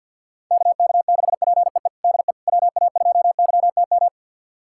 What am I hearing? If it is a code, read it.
Morse code, heard as a keyed tone, about 52 wpm, 700 Hz. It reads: KK6PEE DE WA2YTM
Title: USA Callsign DE Callsign - 6 Characters - 50wpm